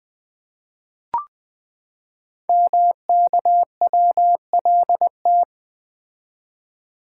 Morse code, heard as a keyed tone, about 20 wpm, 700 Hz. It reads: MKWLT
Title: Code Groups - 5 Letter:Numbers - 20wpm